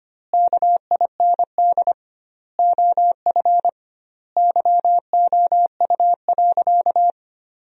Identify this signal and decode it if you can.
Morse code, heard as a keyed tone, about 25 wpm, 700 Hz. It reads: KIND OF YOU.
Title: 2x Word-Spacing - Sentences from Top 200 Words - 25wpm